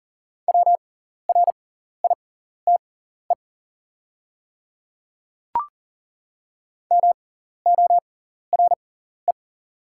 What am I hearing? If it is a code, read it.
Morse code, heard as a keyed tone, about 41 wpm, spaced as for 14 wpm, 700 Hz. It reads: WRITE MORE
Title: 6x Character Spacing - Top 200 Words - 40wpm